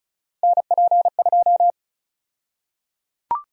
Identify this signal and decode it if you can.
Morse code, heard as a keyed tone, about 36 wpm, 700 Hz. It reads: NP2
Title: USA Call Sign Prefixes - Repeated 2x - 35wpm